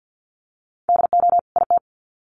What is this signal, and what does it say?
Morse code, heard as a keyed tone, about 50 wpm, 700 Hz. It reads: DO IT
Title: Sets of 2 Words 50wpm